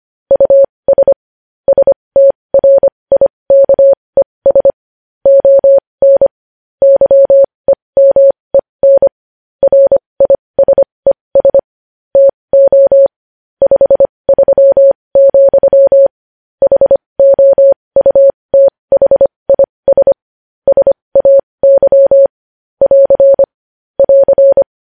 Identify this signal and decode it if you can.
Morse code, heard as a keyed tone, about 25 wpm, spaced as for 20 wpm, 550 Hz. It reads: US STRIKES ON YEMEN RISES TO 53, HOUTHIS SAY <AR> <AR>
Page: News Headlines in Morse Code at 20 WPM